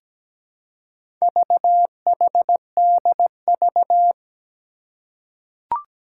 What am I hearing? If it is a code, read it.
Morse code, heard as a keyed tone, about 17 wpm, 700 Hz. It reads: VHDV
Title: Four Challenging Letter-Number Combinations 17wpm